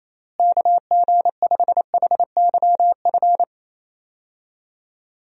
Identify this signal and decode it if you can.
Morse code, heard as a keyed tone, about 28 wpm, 700 Hz. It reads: KG5HYF
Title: USA Call Signs - 6 Characters - Repeated 3x - 28wpm